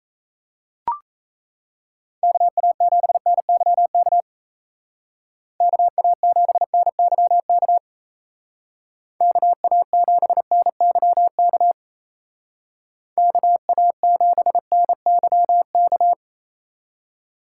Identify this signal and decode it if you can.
Morse code, heard as a keyed tone, about 34 wpm, 700 Hz. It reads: KA7NYK KA7NYK KA7NYK KA7NYK
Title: Speed Racing - USA 2x3 Call Signs 28wpm